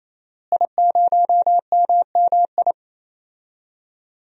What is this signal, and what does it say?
Morse code, heard as a keyed tone, about 28 wpm, 700 Hz. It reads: I0MMI
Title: DXCC-100 Call Signs - 5 Characters - Repeated 3x - 28wpm